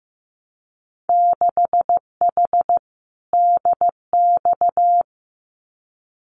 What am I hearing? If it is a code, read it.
Morse code, heard as a keyed tone, about 15 wpm, 700 Hz. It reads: 6H DX
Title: Winter Field Day Exchanges Encore 15wpm